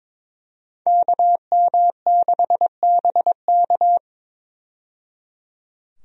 Morse code, heard as a keyed tone, about 22 wpm, 700 Hz. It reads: KM6BK
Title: USA Call Signs - 5 Characters - 22wpm